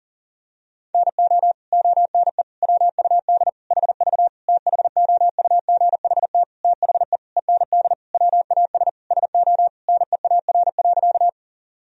Morse code, heard as a keyed tone, about 37 wpm, 700 Hz. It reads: NO ONE WUD HV THOUGHT THE END WAS SO NEAR.
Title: Sentences from Top 300 Words and Abbreviations 40wpm